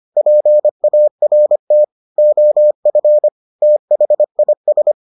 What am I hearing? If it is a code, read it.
Morse code, heard as a keyed tone, about 25 wpm, 600 Hz. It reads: PART OF THIS